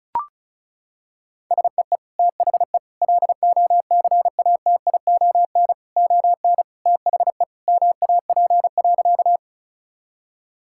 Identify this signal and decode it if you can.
Morse code, heard as a keyed tone, about 36 wpm, 700 Hz. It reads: SEE THE LOCATION ON THE MAP.
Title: Sentences from Top 1000 Words - Review Part 6 - 35wpm